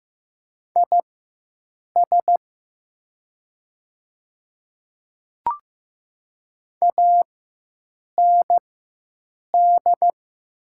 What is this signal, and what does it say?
Morse code, heard as a keyed tone, about 15 wpm, spaced as for 7 wpm, 700 Hz. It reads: IS AND
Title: 4x Character Spacing - Top 100 Words - 15wpm